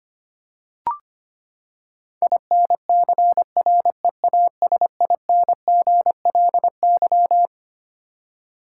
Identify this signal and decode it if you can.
Morse code, heard as a keyed tone, about 25 wpm, 700 Hz. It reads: INCREASINGLY
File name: Twelve Letter Words - 25wpm